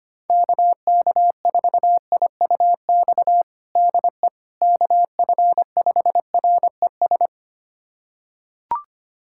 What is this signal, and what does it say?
Morse code, heard as a keyed tone, about 25 wpm, 700 Hz. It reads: KK4IUX DE KF5RES
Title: USA Callsign DE Callsign 25wpm